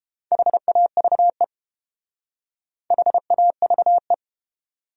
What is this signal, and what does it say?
Morse code, heard as a keyed tone, about 32 wpm, 700 Hz. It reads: HAVE HAVE